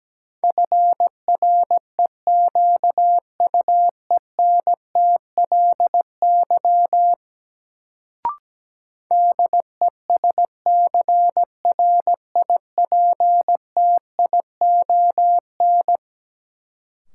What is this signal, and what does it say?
Morse code, heard as a keyed tone, about 17 wpm, 700 Hz. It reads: FREQUENTLY DESCRIPTION
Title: Sentences from Top 1300 Words - New Word Drill - 17wpm